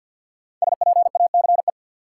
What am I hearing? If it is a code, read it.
Morse code, heard as a keyed tone, about 47 wpm, 700 Hz. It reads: SPACE
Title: Sentences from Top 1500 Words - New Word Drill Alt - 50wpm